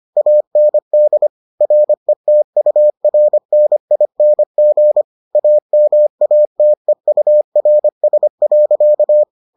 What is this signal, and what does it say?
Morse code, heard as a keyed tone, about 25 wpm, 600 Hz. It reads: AND RETURNING AMATEURS.